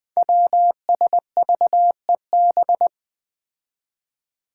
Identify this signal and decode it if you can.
Morse code, heard as a keyed tone, about 20 wpm, 700 Hz. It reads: WSVEB
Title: Five Challenging Letter-Number Combinations 20wpm